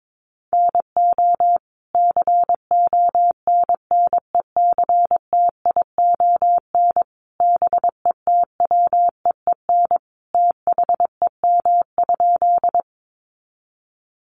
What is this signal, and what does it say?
Morse code, heard as a keyed tone, about 22 wpm, 700 Hz. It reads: NO CONNECTION BETWEEN THEM?
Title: Sentences from Top 1300 Words 22wpm